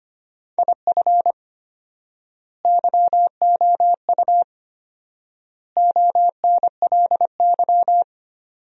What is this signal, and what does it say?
Morse code, heard as a keyed tone, about 25 wpm, 700 Hz. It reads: IF YOU ONLY
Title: 4x Word-Spacing - Sentences from Top 400 Words - 25wpm